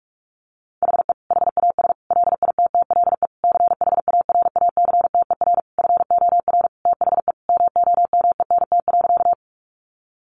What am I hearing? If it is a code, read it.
Morse code, heard as a keyed tone, about 44 wpm, 700 Hz. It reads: HE HAS LITTLE CHARACTER FOR THE MOMENT.
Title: Sentences from Top 1200 Words 45wpm